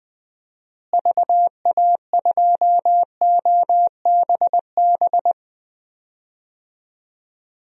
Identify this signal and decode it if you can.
Morse code, heard as a keyed tone, about 20 wpm, 700 Hz. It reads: VA2OBB
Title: DXCC-100 Call Signs - 6 Characters - Repeated 3x - 20wpm